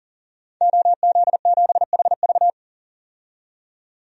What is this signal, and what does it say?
Morse code, heard as a keyed tone, about 40 wpm, 700 Hz. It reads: OZ7HV
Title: DXCC-100 Call Signs - 5 Characters - 40wpm